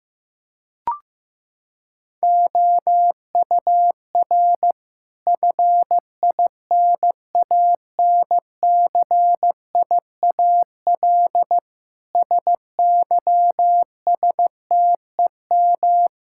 Morse code, heard as a keyed tone, about 15 wpm, 700 Hz. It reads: OUR FINANCIAL SYSTEM
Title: Sentences from Top 1000 Words - Review Part 5 - 15wpm